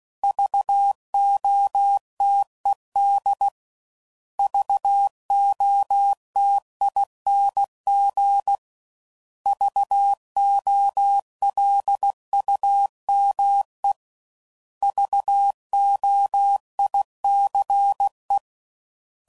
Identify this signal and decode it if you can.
Morse code, heard as a keyed tone, about 16 wpm, 800 Hz. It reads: VOTED VOTING VOLUME VOICE
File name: Prefixes / By First Letter / Vs